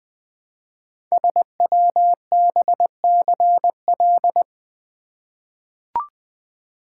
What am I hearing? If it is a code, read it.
Morse code, heard as a keyed tone, about 20 wpm, 700 Hz. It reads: SWBCL
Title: Code Groups - 5 Letters - 20wpm